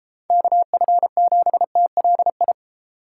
Morse code, heard as a keyed tone, about 33 wpm, 700 Hz. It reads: KF7TLI
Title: Speed Racing - USA 2x3 Call Signs 28wpm